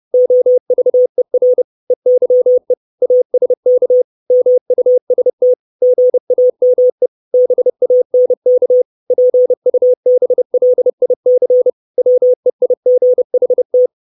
Morse code, heard as a keyed tone, about 30 wpm, 500 Hz. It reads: OVER EYE ASK MUST GAME BANK PUBLIC WEIGHT
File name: 632 Words 30WPM